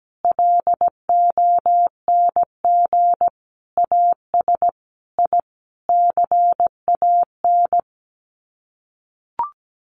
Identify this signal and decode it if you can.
Morse code, heard as a keyed tone, about 17 wpm, 700 Hz. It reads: LONG AS I CAN
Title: Sets of 5 Words Encore2 17wpm